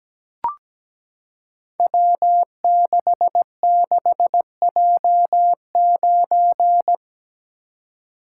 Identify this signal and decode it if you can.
Morse code, heard as a keyed tone, about 17 wpm, 700 Hz. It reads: W66J9
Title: Code Groups - 5 Letter:Numbers - 17wpm